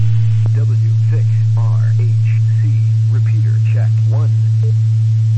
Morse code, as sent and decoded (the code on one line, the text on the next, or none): none